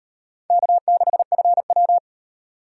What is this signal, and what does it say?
Morse code, heard as a keyed tone, about 35 wpm, 700 Hz. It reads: K6FW